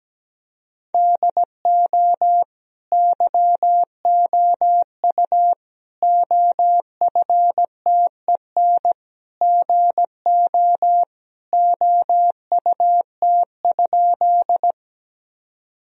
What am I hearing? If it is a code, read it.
Morse code, heard as a keyed tone, about 17 wpm, 700 Hz. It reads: DO YOU OFTEN GO OUT?
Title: Sentences from Top 1000 Words - Review Part 1 - 17wpm